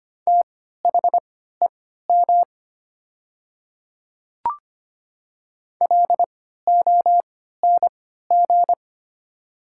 Morse code, heard as a keyed tone, about 25 wpm, spaced as for 14 wpm, 700 Hz. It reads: THEM LONG
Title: 3x Character Spacing - Top 100 Words - 25wpm